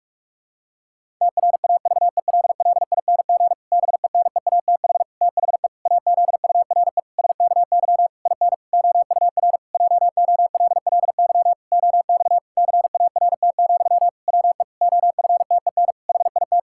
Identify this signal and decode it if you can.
Morse code, heard as a keyed tone, about 46 wpm, 700 Hz. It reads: TRAVELLING BENEATH THE AZURE SKY IN OUR JOLLY OX CART, WE OFTEN HIT
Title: Pangram Challenge v2 - 45wpm